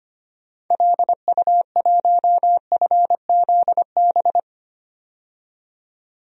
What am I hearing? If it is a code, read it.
Morse code, heard as a keyed tone, about 25 wpm, 700 Hz. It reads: LU1FZB
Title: DXCC-100 Call Signs - 6 Characters - 25wpm